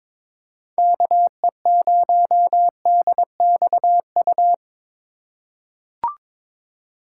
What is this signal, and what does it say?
Morse code, heard as a keyed tone, about 22 wpm, 700 Hz. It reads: KE0DXU
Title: USA Call Signs - 6 Characters - 2x Character Spacing - 22wpm